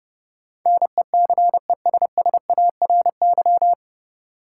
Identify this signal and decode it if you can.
Morse code, heard as a keyed tone, about 32 wpm, 700 Hz. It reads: NECESSARY
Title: Sentences from Top 800 Words - New Word Drill - 30wpm